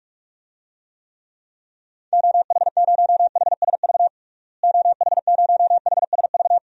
Morse code, heard as a keyed tone, about 46 wpm, 700 Hz. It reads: OH0HSV OH0HSV
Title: DXCC-100 Call Signs - 6 Characters - Repeated 2x - 45wpm